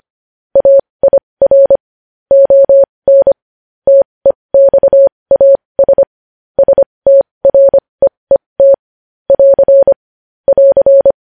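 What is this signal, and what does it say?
Morse code, heard as a keyed tone, about 25 wpm, spaced as for 20 wpm, 550 Hz. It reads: AIR ON TEXAS STREET <AR> <AR>
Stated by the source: Morse code news headlines at 20 wpm.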